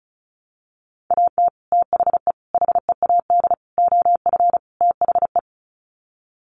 Morse code, heard as a keyed tone, about 34 wpm, 700 Hz. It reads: AT THE HEAD OF THE
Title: Sets of 5 Words 35wpm